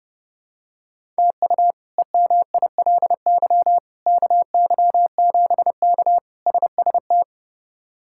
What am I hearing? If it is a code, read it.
Morse code, heard as a keyed tone, about 30 wpm, 700 Hz. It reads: TU EMILY KY7K SST
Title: K1USN SST QSO Practice - 30wpm